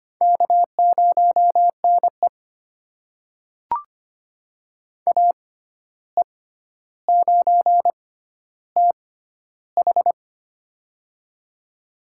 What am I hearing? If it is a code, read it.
Morse code, heard as a keyed tone, about 25 wpm, spaced as for 12 wpm, 700 Hz. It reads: K0NE AE9TH
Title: USA Call Signs - 6x Character Spacing - 25wpm